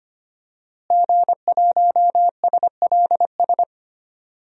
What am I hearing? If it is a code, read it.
Morse code, heard as a keyed tone, about 25 wpm, 700 Hz. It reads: G1SLS
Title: DXCC-100 Call Signs - 5 Characters - Repeated 2x - 25wpm